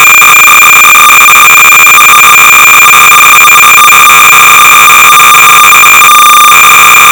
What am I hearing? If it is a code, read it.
Cassette data (Kansas City type).